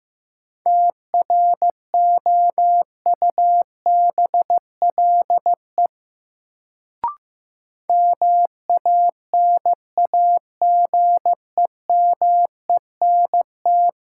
Morse code, heard as a keyed tone, about 15 wpm, 700 Hz. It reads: TROUBLE MANAGEMENT
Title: Sentences from Top 1200 Words - New Word Drill Alt - 15wpm